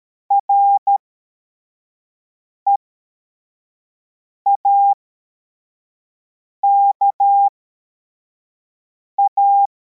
Morse code, heard as a keyed tone, about 13 wpm, spaced as for 4 wpm, 800 Hz. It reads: REAKA